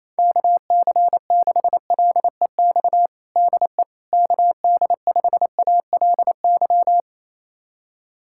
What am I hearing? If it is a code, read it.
Morse code, heard as a keyed tone, about 28 wpm, 700 Hz. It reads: KC6LEX DE KD5ALY
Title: USA Callsign DE Callsign 28wpm